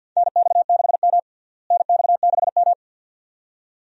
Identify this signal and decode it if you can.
Morse code, heard as a keyed tone, about 53 wpm, 700 Hz. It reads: NX6M NX6M
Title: USA Call Signs - 4 Characters - Repeated 2x - 50wpm